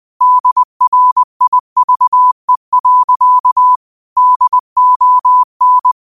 Morse code, heard as a keyed tone, about 20 wpm, 1000 Hz. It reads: DRIVE. DON
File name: qotdfiles / 20wpm